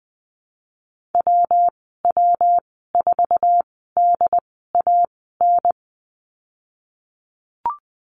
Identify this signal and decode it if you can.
Morse code, heard as a keyed tone, about 20 wpm, spaced as for 14 wpm, 700 Hz. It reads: WW4DAN
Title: 173 Lesson 28g - Number 9 - Call Signs 20wpm